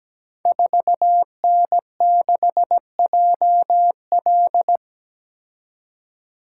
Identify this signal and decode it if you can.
Morse code, heard as a keyed tone, about 17 wpm, 700 Hz. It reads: 4N6JL
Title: Code Groups - 5 Letter:Numbers - 17wpm